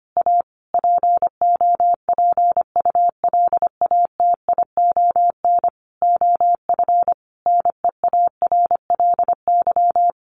Morse code, heard as a keyed tone, about 25 wpm, 700 Hz. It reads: A POPULATION OF NEARLY
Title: Sentences from Top 1300 Words 25wpm